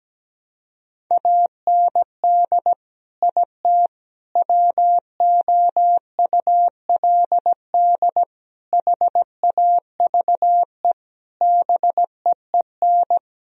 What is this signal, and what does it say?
Morse code, heard as a keyed tone, about 17 wpm, 700 Hz. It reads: AND IT WOULD HAVE BEEN
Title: Sentences from Top 1000 Words - Review Part 4 - 17wpm